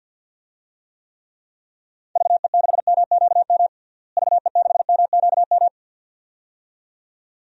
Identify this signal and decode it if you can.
Morse code, heard as a keyed tone, about 47 wpm, 700 Hz. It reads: VE6MQM VE6MQM
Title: DXCC-100 Call Signs - 6 Characters - Repeated 2x - 50wpm